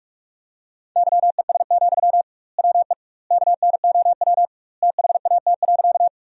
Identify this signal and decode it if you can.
Morse code, heard as a keyed tone, about 46 wpm, 700 Hz. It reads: YES, WE KNOW THAT.
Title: 2x Word-Spacing - Sentences from Top 500 Words - 45wpm